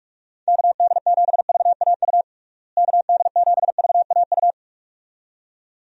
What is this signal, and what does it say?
Morse code, heard as a keyed tone, about 46 wpm, 700 Hz. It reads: KD7VAU KD7VAU
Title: USA Call Signs - 6 Characters - Repeated 2x - 45wpm